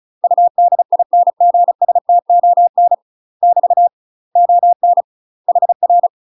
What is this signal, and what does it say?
Morse code, heard as a keyed tone, about 35 wpm, 700 Hz. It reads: UDINGSTON <BT> ON HR
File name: UK MASTER QSOs TRAINING / 35WPM / UK MASTER QSOs 0033